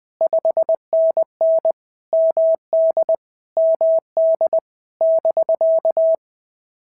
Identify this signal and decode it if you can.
Morse code, heard as a keyed tone, about 19 wpm, 650 Hz. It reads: 5NN MD MD <BK>
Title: POTA QSO Practice - 20wpm